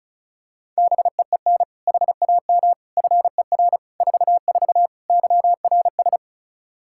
Morse code, heard as a keyed tone, about 36 wpm, 700 Hz. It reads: BEEN HAM FER 44 YRS
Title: Practice QSO - Exchange 3 - More Info - 35wpm